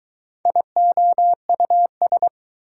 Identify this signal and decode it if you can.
Morse code, heard as a keyed tone, about 24 wpm, 700 Hz. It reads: IOUS